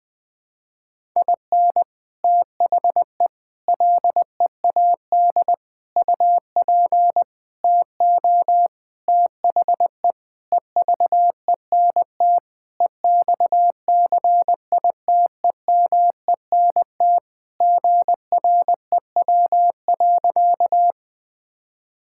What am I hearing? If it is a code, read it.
Morse code, heard as a keyed tone, about 21 wpm, 700 Hz. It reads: IN THE LEAD UP TO THE EVENT EXCITEMENT GREW.
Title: Sentences with 6-Word Phrases - Spoken Beginning - 20wpm